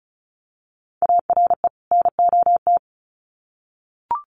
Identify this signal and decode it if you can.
Morse code, heard as a keyed tone, about 34 wpm, 700 Hz. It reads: ARE NOT
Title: Sets of 2 Words 35wpm